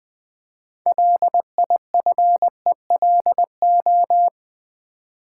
Morse code, heard as a keyed tone, about 20 wpm, 700 Hz. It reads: LIFELO
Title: Build A Word - 3 Letters At A Time - 8-Letter Words - 20wpm